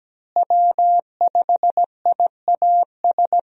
Morse code, heard as a keyed tone, about 17 wpm, 700 Hz. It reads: W5IAS
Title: USA Call Signs - 5 Characters - Repeated 3x - 17wpm